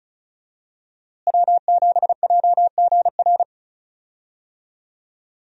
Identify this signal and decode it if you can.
Morse code, heard as a keyed tone, about 36 wpm, 700 Hz. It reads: W7JGR
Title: USA Call Signs - 5 Characters - Repeated 3x - 35wpm